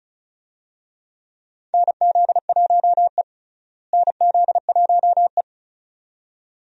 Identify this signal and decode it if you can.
Morse code, heard as a keyed tone, about 36 wpm, 700 Hz. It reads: NZ1E NZ1E
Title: USA Call Signs - 4 Characters - Repeated 2x - 35wpm